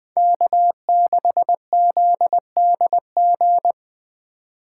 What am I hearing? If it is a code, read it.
Morse code, heard as a keyed tone, about 24 wpm, 700 Hz. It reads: K6ZDG
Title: Speed Racing - USA 1x3 Call Signs 20wpm